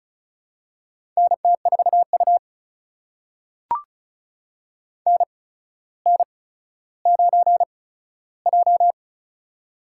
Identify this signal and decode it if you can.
Morse code, heard as a keyed tone, about 36 wpm, spaced as for 15 wpm, 700 Hz. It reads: NT4U NN9J
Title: USA Call Signs - 4 Characters - 8x Character Spacing - 35wpm